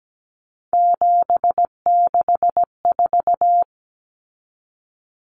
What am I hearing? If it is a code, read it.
Morse code, heard as a keyed tone, about 17 wpm, 700 Hz. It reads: 764
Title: Numbers - 3-Digit 17wpm